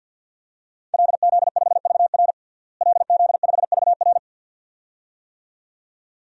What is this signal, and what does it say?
Morse code, heard as a keyed tone, about 51 wpm, 700 Hz. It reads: LZ5VR LZ5VR
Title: DXCC-100 Call Signs - 5 Characters - Repeated 2x - 50wpm